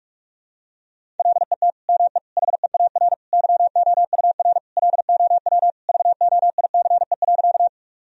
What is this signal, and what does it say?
Morse code, heard as a keyed tone, about 47 wpm, 700 Hz. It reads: LET ME HEAR YOUR LOW VOICE.
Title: Sentences from Top 1000 Words - Review Part 1 - 45wpm